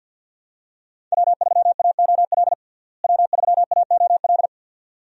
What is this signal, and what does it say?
Morse code, heard as a keyed tone, about 51 wpm, 700 Hz. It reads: W3AOL W3AOL
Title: USA Call Signs - 5 Characters - Repeated 2x - 50wpm